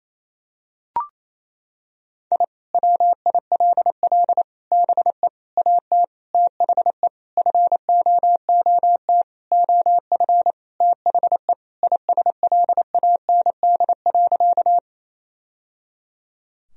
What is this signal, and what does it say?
Morse code, heard as a keyed tone, about 28 wpm, 700 Hz. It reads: I WILL BE AT THE FOOT OF THE ISLAND.
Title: Sentences from Top 1000 Words - Review Part 3 - 28wpm